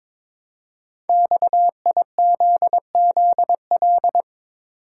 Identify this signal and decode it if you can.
Morse code, heard as a keyed tone, about 22 wpm, 700 Hz. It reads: XIZZL
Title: Code Groups - 5 Letters - 22wpm